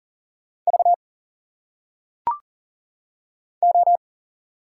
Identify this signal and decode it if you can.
Morse code, heard as a keyed tone, about 40 wpm, 700 Hz. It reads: V O
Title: Single Letters - Phonetic Pronunciation 40wpm